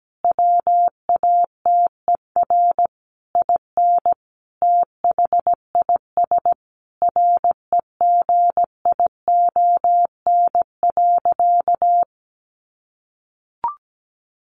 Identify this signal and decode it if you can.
Morse code, heard as a keyed tone, about 17 wpm, 700 Hz. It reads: WATER IN THIS REGION.
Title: Sentences from Top 1500 Words 17wpm